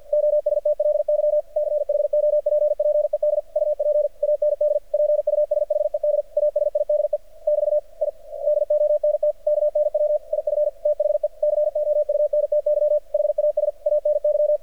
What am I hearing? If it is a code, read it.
Morse code, heard as a keyed tone, about 51 wpm, 600 Hz. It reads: OSTLY CHOPPED UP AND PUSHED ASIDE <BT> I DONT KNW IF THE COUNTY HAS ANY
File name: archived / CRSnet